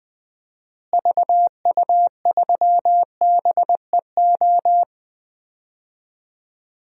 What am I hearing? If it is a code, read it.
Morse code, heard as a keyed tone, about 20 wpm, 700 Hz. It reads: VU3BEO
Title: DXCC-100 Call Signs - 6 Characters - Repeated 2x - 20wpm